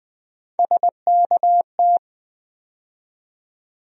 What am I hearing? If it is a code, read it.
Morse code, heard as a keyed tone, about 20 wpm, 700 Hz. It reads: SKT
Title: USA Call Sign Postfixes - 3 Letters 20wpm